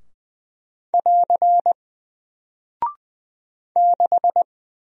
Morse code, heard as a keyed tone, about 20 wpm, 700 Hz. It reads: <AR> 6
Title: Single Character (Letter:Number:Punctuation:Prosign) Alt 20wpm